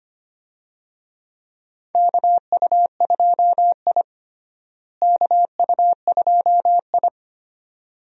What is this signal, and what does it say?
Morse code, heard as a keyed tone, about 25 wpm, 700 Hz. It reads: KU2I KU2I
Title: USA Call Signs - 4 Characters - Repeated 2x - 25wpm